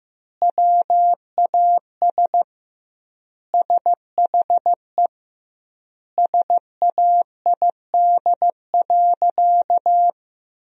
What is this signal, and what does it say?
Morse code, heard as a keyed tone, about 15 wpm, 700 Hz. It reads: WAS SHE SAID.
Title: 2x Word-Spacing - Sentences from Top 400 Words - 15wpm